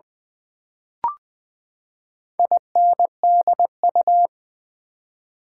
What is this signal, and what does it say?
Morse code, heard as a keyed tone, about 20 wpm, 700 Hz. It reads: INDU
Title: Build A Word - 4 Letters At A Time - 10-Letter Words - 20wpm